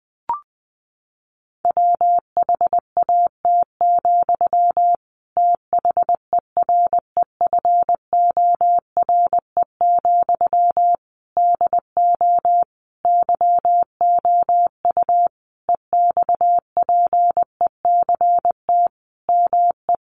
Morse code, heard as a keyed tone, about 20 wpm, 700 Hz. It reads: WHAT, THEREFORE, DO YOU EXPECT ME
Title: Sentences from Top 1100 Words 20wpm